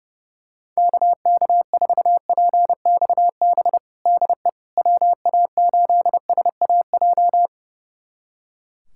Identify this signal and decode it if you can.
Morse code, heard as a keyed tone, about 30 wpm, 700 Hz. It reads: KK4PXB DE WA8SAJ
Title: USA Callsign DE Callsign - 6 Characters - 30wpm